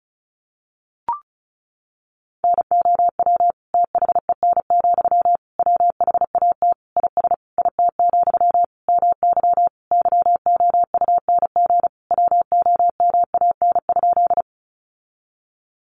Morse code, heard as a keyed tone, about 34 wpm, 700 Hz. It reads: NOW THEN, WHAT IS IT, MY YOUNG WOMAN?
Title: Sentences from Top 1100 Words 35wpm